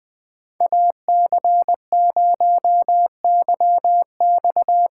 Morse code, heard as a keyed tone, about 20 wpm, 700 Hz. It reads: AC0YX